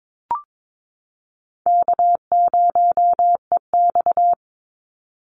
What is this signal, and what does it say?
Morse code, heard as a keyed tone, about 22 wpm, 700 Hz. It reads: K0EX
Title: Easy Call Signs Repeated 2x - 22wpm